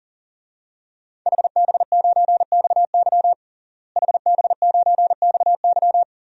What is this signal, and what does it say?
Morse code, heard as a keyed tone, about 37 wpm, 700 Hz. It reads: HB9XY HB9XY
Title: DXCC-100 Call Signs - 5 Characters - Repeated 3x - 40wpm